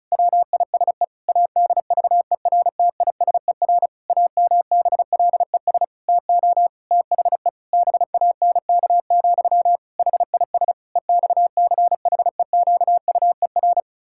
Morse code, heard as a keyed tone, about 36 wpm, 700 Hz. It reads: WISE ADVERTISER AMBLES TO THE BANK, HIS EXCHEQUER
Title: Pangram Challenge v2 - 35wpm